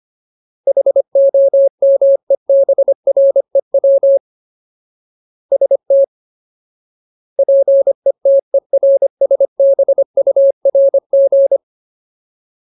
Morse code, heard as a keyed tone, about 25 wpm, 550 Hz. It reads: HOMEBREW ST PETERSBURG
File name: Ultimate Word List / 25WPM Ultimate